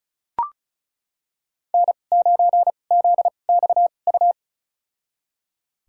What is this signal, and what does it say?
Morse code, heard as a keyed tone, about 34 wpm, spaced as for 25 wpm, 700 Hz. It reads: N9ZXU
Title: USA Call Signs - 2x Character Spacing - 35wpm